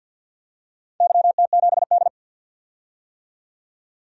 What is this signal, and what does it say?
Morse code, heard as a keyed tone, about 47 wpm, 700 Hz. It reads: YT7D
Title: DXCC-100 Call Signs - 4 Characters - 50wpm